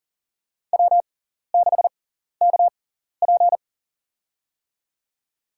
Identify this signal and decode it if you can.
Morse code, heard as a keyed tone, about 40 wpm, spaced as for 14 wpm, 700 Hz. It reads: W6KP
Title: USA Call Signs - 4 Characters - 6x Character Spacing - 40wpm